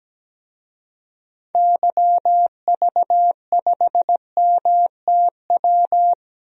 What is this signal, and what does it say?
Morse code, heard as a keyed tone, about 17 wpm, 700 Hz. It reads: YV5MTW
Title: DXCC-100 Call Signs - 6 Characters - Repeated 2x - 17wpm